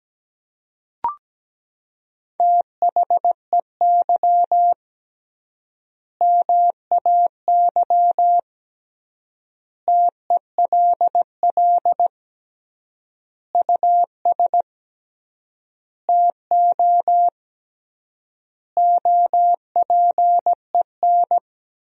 Morse code, heard as a keyed tone, about 17 wpm, 700 Hz. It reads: THEY MAY TELL US TO OPEN
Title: 3x Word-Spacing - Sentences from Top 300 Words - 17wpm